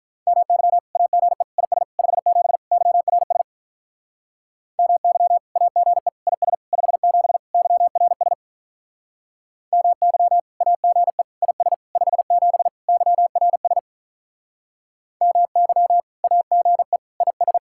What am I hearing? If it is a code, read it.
Morse code, heard as a keyed tone, about 43 wpm, 700 Hz. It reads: MY AGE IS 57 YRS MY AGE IS 57 YRS MY AGE IS 57 YRS MY AGE IS
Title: Speed-Racing Common QSO Phrases Encore - 35wpm